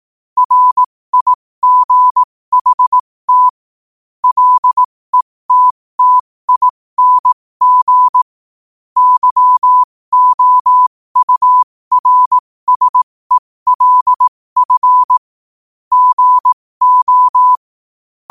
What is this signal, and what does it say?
Morse code, heard as a keyed tone, about 18 wpm, spaced as for 15 wpm, 1000 Hz. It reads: RIGHT LETTING YOURSELF GO
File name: qotdfiles / 15wpm